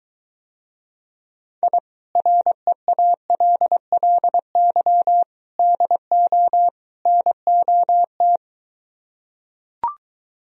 Morse code, heard as a keyed tone, about 23 wpm, 700 Hz. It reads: I REALLY DO NOT